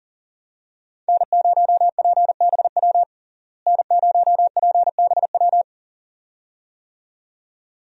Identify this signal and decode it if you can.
Morse code, heard as a keyed tone, about 40 wpm, 700 Hz. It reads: N0PBW N0PBW
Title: USA Call Signs - 5 Characters - Repeated 2x - 40wpm